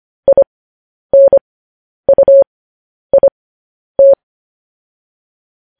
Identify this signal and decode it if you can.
Morse code, heard as a keyed tone, about 25 wpm, spaced as for 10 wpm, 550 Hz. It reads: INUIT
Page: news headlines in morse code at 10 wpm